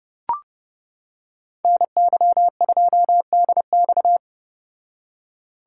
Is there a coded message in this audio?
Yes: Morse code from 1.6 to 4.2 s, heard as a keyed tone, about 30 wpm, 700 Hz. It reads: NY2DX